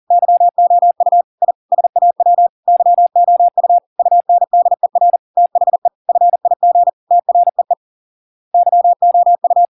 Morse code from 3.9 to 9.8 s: UNDER THE FIG TREE YOU